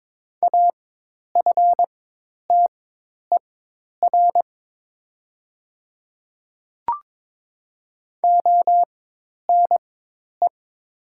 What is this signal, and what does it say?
Morse code, heard as a keyed tone, about 21 wpm, spaced as for 10 wpm, 700 Hz. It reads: AFTER ONE
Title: 4x Character Spacing - Top 200 Words - 22wpm